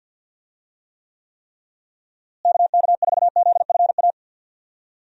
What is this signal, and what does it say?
Morse code, heard as a keyed tone, about 51 wpm, 700 Hz. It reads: KK4ZFA
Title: USA Call Signs - 6 Characters - Repeated 3x - 50wpm